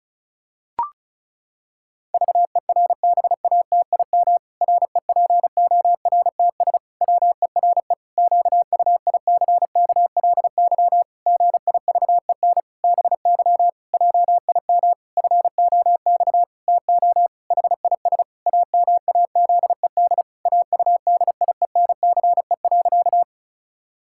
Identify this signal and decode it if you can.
Morse code, heard as a keyed tone, about 34 wpm, 700 Hz. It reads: VERBATIM REPORTS WERE QUICKLY GIVEN BY JIM FOX TO HIS AMAZED AUDIENCE.
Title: Pangram Challenge v2 - 35wpm